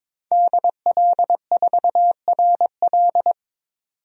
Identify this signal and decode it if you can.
Morse code, heard as a keyed tone, about 22 wpm, 700 Hz. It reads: DL4RL
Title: DXCC-100 Call Signs - 5 Characters - 22wpm